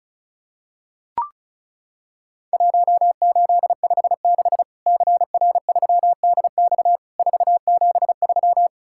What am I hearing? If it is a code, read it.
Morse code, heard as a keyed tone, about 34 wpm, 700 Hz. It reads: 1856 CR3DX 473
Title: Worked All Europe DX Contest - QTC Exchange - 35wpm